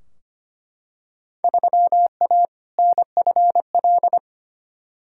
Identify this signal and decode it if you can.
Morse code, heard as a keyed tone, about 25 wpm, 700 Hz. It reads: 3A NFL
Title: Field Day Exchanges - 25wpm